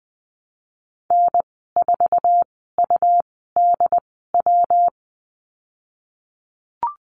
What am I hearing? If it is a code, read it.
Morse code, heard as a keyed tone, about 20 wpm, spaced as for 15 wpm, 700 Hz. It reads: N4UDW
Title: 179 Lesson 29g - Slash - Call Signs 20wpm